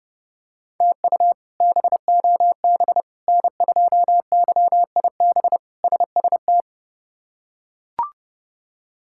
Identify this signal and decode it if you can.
Morse code, heard as a keyed tone, about 32 wpm, 700 Hz. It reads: TU BOB N2YIB SST
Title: K1USN SST QSO Practice - 30wpm